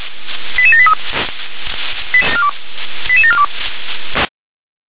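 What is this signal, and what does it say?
Five-tone selective call (ZVEI), heard as quick runs of five tones, tones of about 70 ms. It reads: 88783 88742
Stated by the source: ZVEI five-tone call 88730 received by SDR.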